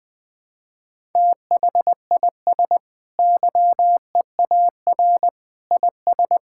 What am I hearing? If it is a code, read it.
Morse code, heard as a keyed tone, about 21 wpm, 700 Hz. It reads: THIS YEAR IS
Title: Sentences with 2-Word Phrases - 20wpm